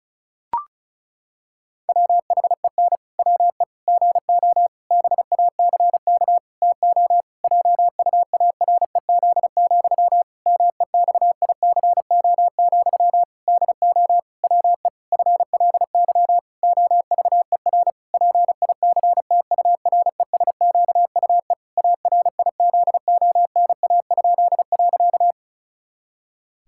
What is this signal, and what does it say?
Morse code, heard as a keyed tone, about 36 wpm, 700 Hz. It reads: WHEN WE GO BACK TO JUAREZ, MEXICO, DO WE FLY OVER PICTURESQUE ARIZONA?.
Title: Pangram Challenge v2 - 35wpm